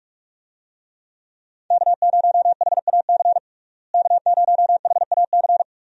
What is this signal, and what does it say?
Morse code, heard as a keyed tone, about 42 wpm, 700 Hz. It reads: K0HAC K0HAC
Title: USA Call Signs - 5 Characters - Repeated 3x - 45wpm